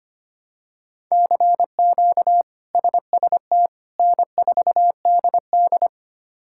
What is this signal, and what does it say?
Morse code, heard as a keyed tone, about 25 wpm, 700 Hz. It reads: CQ SST N4DD
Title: K1USN SST QSO Practice - 25wpm